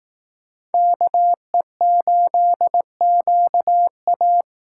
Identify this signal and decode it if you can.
Morse code, heard as a keyed tone, about 19 wpm, 700 Hz. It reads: KE8QA
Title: Speed Racing - USA Call Signs 15wpm